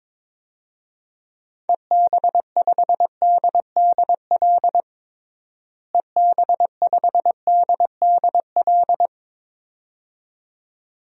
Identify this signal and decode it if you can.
Morse code, heard as a keyed tone, about 22 wpm, 700 Hz. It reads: EB5DDL EB5DDL
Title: DXCC-100 Call Signs - 6 Characters - Repeated 2x - 22wpm